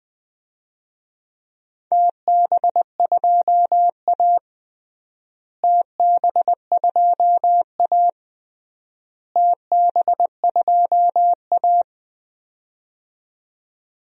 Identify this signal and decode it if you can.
Morse code, heard as a keyed tone, about 20 wpm, 700 Hz. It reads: TB2A TB2A TB2A
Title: DXCC-100 Call Signs - 4 Characters - Repeated 3x - 20wpm